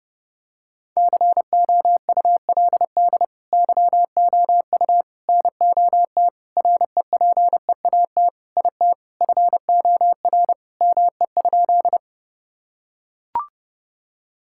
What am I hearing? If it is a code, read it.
Morse code, heard as a keyed tone, about 32 wpm, 700 Hz. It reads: COULD YOU NOT REPEAT IT FOR ME?
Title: Sentences from Top 1600 Words 30wpm